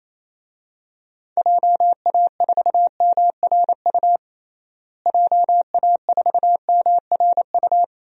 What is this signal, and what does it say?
Morse code, heard as a keyed tone, about 27 wpm, 700 Hz. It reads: JA4MRU JA4MRU
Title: DXCC-100 Call Signs - 6 Characters - Repeated 2x - 28wpm